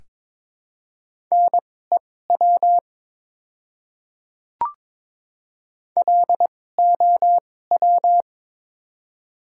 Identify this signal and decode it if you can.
Morse code, heard as a keyed tone, about 22 wpm, spaced as for 16 wpm, 690 Hz. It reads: NEW LOW
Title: 2x Character Spacing - Top 500 Words - 22wpm